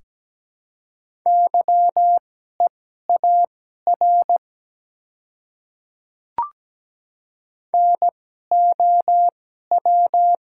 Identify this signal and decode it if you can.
Morse code, heard as a keyed tone, about 17 wpm, spaced as for 12 wpm, 700 Hz. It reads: YEAR NOW
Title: 2x Character Spacing - Top 200 Words - 17wpm